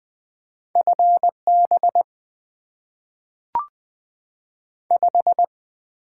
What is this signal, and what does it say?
Morse code, heard as a keyed tone, about 20 wpm, 700 Hz. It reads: F B 5
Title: Top 20 QSO Elements - 8x Character Spacing - 20wpm